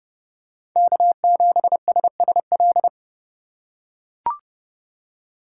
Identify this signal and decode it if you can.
Morse code, heard as a keyed tone, about 30 wpm, 700 Hz. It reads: K7SSL
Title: USA Call Signs - 2x Character Spacing - 30wpm